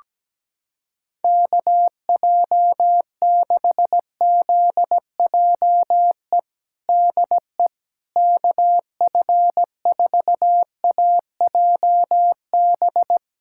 Morse code, heard as a keyed tone, about 17 wpm, 700 Hz. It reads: KJ6ZJE DE KF4AJB
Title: USA Callsign DE Callsign 17wpm